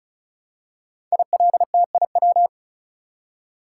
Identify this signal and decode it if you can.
Morse code, heard as a keyed tone, about 36 wpm, 700 Hz. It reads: ILTIW
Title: Code Groups - 5 Letters - 35wpm